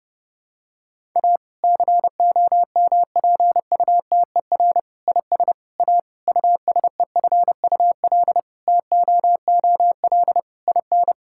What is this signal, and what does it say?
Morse code, heard as a keyed tone, about 30 wpm, 700 Hz. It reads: A COMPUTER IS A USEFUL TOOL IN
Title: Sentences from Top 1000 Words - Review Part 6 - 30wpm